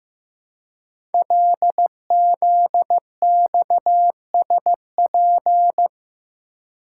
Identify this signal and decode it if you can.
Morse code, heard as a keyed tone, about 15 wpm, 700 Hz. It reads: LZXSP
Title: Code Groups - 5 Letters - 15wpm